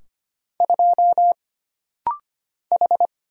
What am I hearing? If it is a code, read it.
Morse code, heard as a keyed tone, about 25 wpm, 700 Hz. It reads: 2 H
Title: Single Letter-Number Alt - Rapid-Fire 25wpm